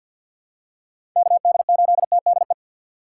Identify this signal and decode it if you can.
Morse code, heard as a keyed tone, about 52 wpm, 700 Hz. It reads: KD8TDE
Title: USA Call Signs - 6 Characters - 4x Character Spacing - 50wpm